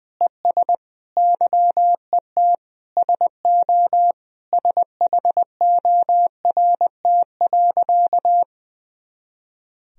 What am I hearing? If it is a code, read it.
Morse code, heard as a keyed tone, about 20 wpm, 700 Hz. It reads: ES YET SO SHORT.
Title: Sentences from Top 500 Words and Abbreviations 20wpm